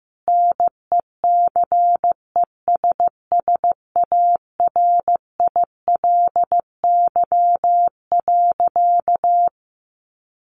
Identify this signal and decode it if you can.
Morse code, heard as a keyed tone, about 15 wpm, 700 Hz. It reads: NECESSARILY.
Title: Sentences from Top 1400 Words 15wpm